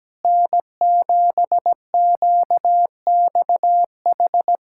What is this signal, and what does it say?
Morse code, heard as a keyed tone, about 17 wpm, 700 Hz. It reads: N7QXH
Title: USA Callsign DE Callsign - 5 Characters - 17wpm